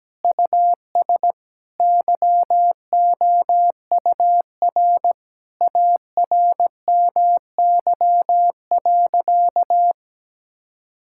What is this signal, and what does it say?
Morse code, heard as a keyed tone, about 17 wpm, 700 Hz. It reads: US YOUR ARMY.
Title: Sentences from Top 1000 Words - Review Part 5 - 17wpm